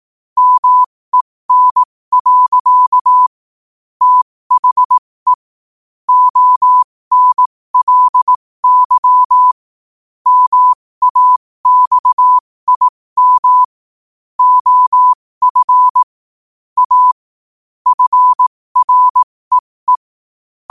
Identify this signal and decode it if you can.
Morse code, heard as a keyed tone, about 18 wpm, spaced as for 15 wpm, 1000 Hz. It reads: MEN. THE ONLY MAXIM OF A FREE
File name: qotdfiles / 15wpm